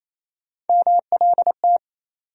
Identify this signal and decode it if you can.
Morse code, heard as a keyed tone, about 28 wpm, 700 Hz. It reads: MLT
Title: USA Call Sign Postfixes - 3 Letters 28wpm